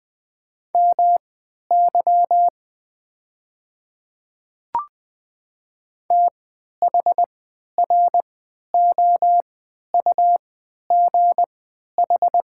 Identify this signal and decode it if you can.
Morse code, heard as a keyed tone, about 20 wpm, spaced as for 11 wpm, 700 Hz. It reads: MY THROUGH
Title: 3x Character Spacing - Top 200 Words - 20wpm